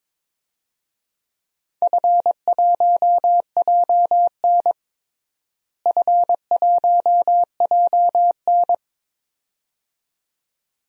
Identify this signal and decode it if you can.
Morse code, heard as a keyed tone, about 22 wpm, 700 Hz. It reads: F1JN F1JN
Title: DXCC-100 Call Signs - 4 Characters - Repeated 2x - 22wpm